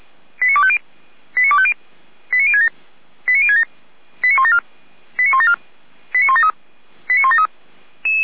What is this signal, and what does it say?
Five-tone selective call (ZVEI), heard as quick runs of five tones, tones of about 70 ms. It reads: 89250 89250 89067 89067 89174 89174 89173 89173